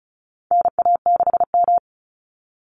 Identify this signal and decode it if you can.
Morse code, heard as a keyed tone, about 34 wpm, 700 Hz. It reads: NA6M
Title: Call Signs Repeated 3x - 35wpm